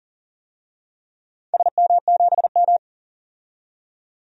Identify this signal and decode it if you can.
Morse code, heard as a keyed tone, about 38 wpm, 700 Hz. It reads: SM7M
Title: DXCC-100 Call Signs - 4 Characters - 40wpm